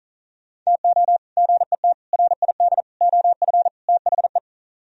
Morse code, heard as a keyed tone, about 47 wpm, 700 Hz. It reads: TO GET RID OF THE